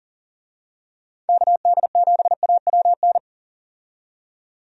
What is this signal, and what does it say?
Morse code, heard as a keyed tone, about 40 wpm, 700 Hz. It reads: KD7AWN
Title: USA Call Signs - 6 Characters - 40wpm